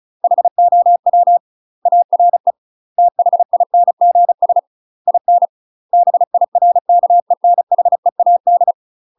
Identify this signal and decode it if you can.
Morse code, heard as a keyed tone, about 35 wpm, 700 Hz. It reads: HOW ARE THINGS IN BIRKENHEAD
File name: UK MASTER QSOs TRAINING / 35WPM / UK MASTER QSOs 0016